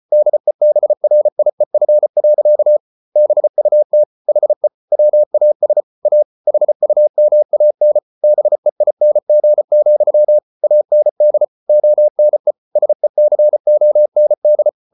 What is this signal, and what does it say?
Morse code, heard as a keyed tone, about 34 wpm, 600 Hz. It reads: DEBRIEF. BUT HE WAS A HUMAN BEING, AND ONE SECOND